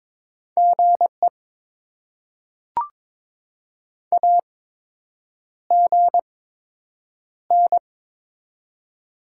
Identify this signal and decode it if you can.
Morse code, heard as a keyed tone, about 22 wpm, spaced as for 8 wpm, 700 Hz. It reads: GE AGN
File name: Top 20 QSO Elements - 8x Character Spacing - 22wpm